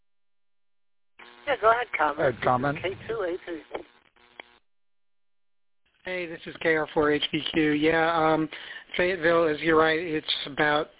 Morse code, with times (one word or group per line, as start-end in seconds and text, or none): none